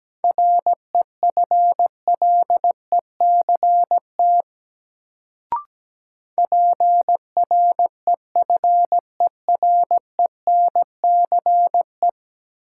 Morse code, heard as a keyed tone, about 17 wpm, 700 Hz. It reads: REFLECT PREFERENCE
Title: Sentences from Top 1500 Words - New Word Drill Alt - 17wpm